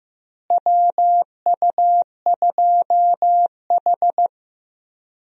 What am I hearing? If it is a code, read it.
Morse code, heard as a keyed tone, about 15 wpm, 700 Hz. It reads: WU2H
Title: USA Call Signs - 4 Characters - Repeated 3x - 15wpm